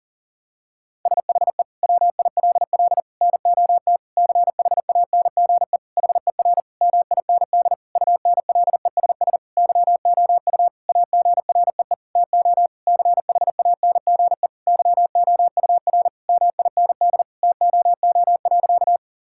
Morse code, heard as a keyed tone, about 40 wpm, 700 Hz. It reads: SHE WILL NOT CHANGE HER MIND UNLESS YOU AGREE TO CHANGE YOUR MIND TOO.
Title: Sentences from Top 1000 Words - Review Part 6 - 40wpm